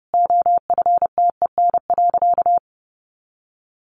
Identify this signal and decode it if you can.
Morse code, heard as a keyed tone, about 30 wpm, 700 Hz. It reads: OFTEN.
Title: A Sound Of Thunder 30wpm